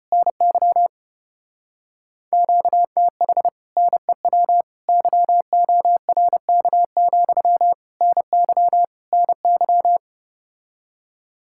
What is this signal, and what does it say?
Morse code, heard as a keyed tone, about 33 wpm, 700 Hz. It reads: NY QTH NEW YORK, NY NY